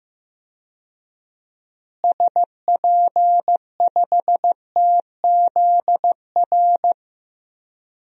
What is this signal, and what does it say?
Morse code, heard as a keyed tone, about 15 wpm, 700 Hz. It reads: SP5TZR
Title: DXCC-100 Call Signs - 6 Characters - Repeated 3x - 15wpm